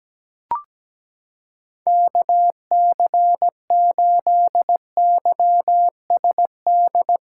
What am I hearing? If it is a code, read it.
Morse code, heard as a keyed tone, about 17 wpm, 700 Hz. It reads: KC8YSD
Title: USA Callsign DE Callsign - 6 Characters - 17wpm